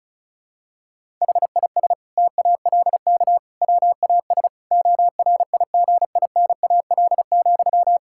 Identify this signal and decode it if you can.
Morse code, heard as a keyed tone, about 36 wpm, 700 Hz. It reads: HIS TALK WAS ORIGINAL,
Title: Sentences from Top 1000 Words - Review Part 6 - 35wpm